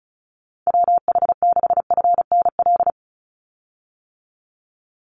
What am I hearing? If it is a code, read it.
Morse code, heard as a keyed tone, about 34 wpm, 700 Hz. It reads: WH6FNL
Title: USA Call Signs 35wpm